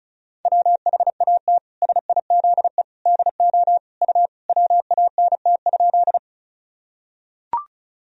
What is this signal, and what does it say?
Morse code, heard as a keyed tone, about 36 wpm, 700 Hz. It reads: WHAT SIZE DO U WANT?
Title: Sentences from Top 400 Words and Abbreviations 35wpm